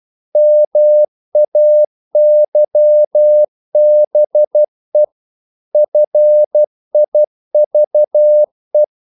Morse code, heard as a keyed tone, about 12 wpm, 600 Hz. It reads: MAYBE FIVE
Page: English quotations in CW (telegraphy - morse)